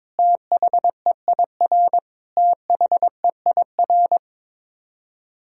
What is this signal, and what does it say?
Morse code, heard as a keyed tone, about 21 wpm, 700 Hz. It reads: THEIR THEIR
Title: Top 100 Words Repeated 22wpm